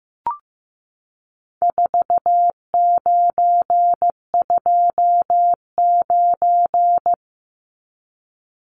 Morse code, heard as a keyed tone, about 15 wpm, 700 Hz. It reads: 4929
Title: Numbers - 4-Digit 15wpm